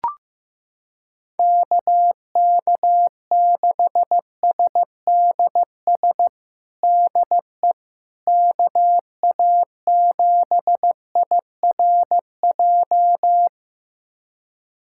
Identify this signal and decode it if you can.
Morse code, heard as a keyed tone, about 15 wpm, 700 Hz. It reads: KK6SDS DE KA7IRJ
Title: USA Callsign DE Callsign - 6 Characters - 15wpm